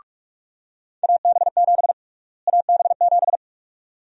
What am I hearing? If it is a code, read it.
Morse code, heard as a keyed tone, about 46 wpm, 700 Hz. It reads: AB7 AB7
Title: USA Call Sign Prefixes - Repeated 2x - 45wpm